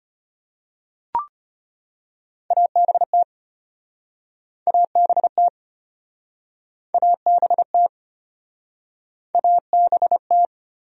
Morse code, heard as a keyed tone, about 34 wpm, 700 Hz. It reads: ABT ABT ABT ABT